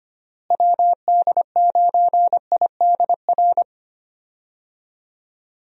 Morse code, heard as a keyed tone, about 25 wpm, 700 Hz. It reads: WD9IDR
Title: USA Call Signs - 6 Characters - Repeated 2x - 25wpm